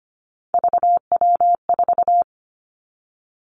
Morse code, heard as a keyed tone, about 25 wpm, 700 Hz. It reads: VW4
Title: Three Challenging Letter-Number Combinations 25wpm